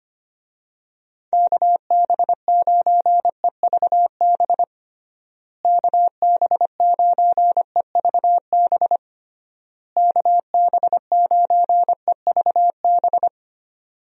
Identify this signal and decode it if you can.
Morse code, heard as a keyed tone, about 25 wpm, 700 Hz. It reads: KB9EVB KB9EVB KB9EVB
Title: USA Call Signs - 6 Characters - Repeated 3x - 25wpm